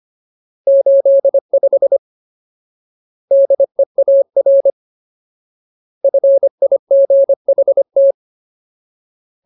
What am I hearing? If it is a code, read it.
Morse code, heard as a keyed tone, about 25 wpm, 550 Hz. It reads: 85 DEAR FIGHT
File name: Ultimate Word List / 25WPM Ultimate